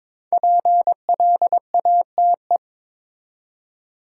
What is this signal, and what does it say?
Morse code, heard as a keyed tone, about 22 wpm, 700 Hz. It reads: PLATE
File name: 5 Letter Words - First 3 Letters Given - 22wpm